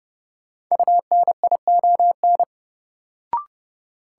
Morse code, heard as a keyed tone, about 32 wpm, 700 Hz. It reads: UNION
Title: Sentences from Top 1400 Words - New Word Drill - 30wpm